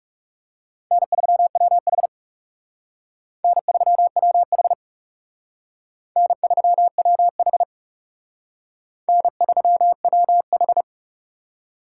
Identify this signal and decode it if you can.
Morse code, heard as a keyed tone, about 38 wpm, 700 Hz. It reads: N3WH N3WH N3WH N3WH